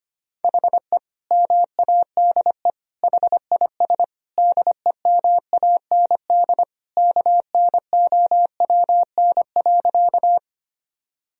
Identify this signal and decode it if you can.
Morse code, heard as a keyed tone, about 25 wpm, 700 Hz. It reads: HE MADE HIS DEMAND KNOWN.
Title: Sentences from Top 1000 Words - Review Part 4 - 25wpm